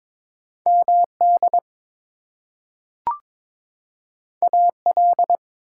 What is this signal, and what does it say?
Morse code, heard as a keyed tone, about 22 wpm, 700 Hz. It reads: MD AL
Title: US States and Canadian Provinces 22wpm